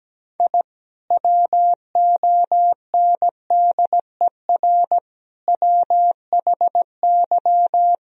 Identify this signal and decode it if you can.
Morse code, heard as a keyed tone, about 17 wpm, 700 Hz. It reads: I WONDER WHY
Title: Sentences from Top 1000 Words - Review Part 3 - 17wpm